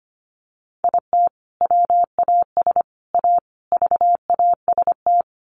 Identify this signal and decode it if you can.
Morse code, heard as a keyed tone, about 25 wpm, 700 Hz. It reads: IT WAS A VAST
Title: Sentences from Top 1300 Words 25wpm